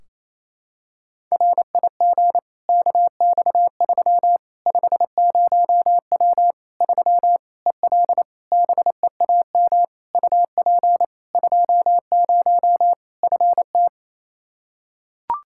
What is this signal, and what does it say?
Morse code, heard as a keyed tone, about 28 wpm, 700 Hz. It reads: RIG KX3 50W 3 EL BEAM UP 20 FT
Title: Practice QSO - Exchange 2 - Info - 28wpm